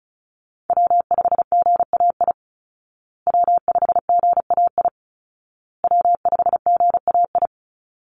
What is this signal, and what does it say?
Morse code, heard as a keyed tone, about 34 wpm, 700 Hz. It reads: W5GAI W5GAI W5GAI
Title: Call Signs Repeated 2x - 35wpm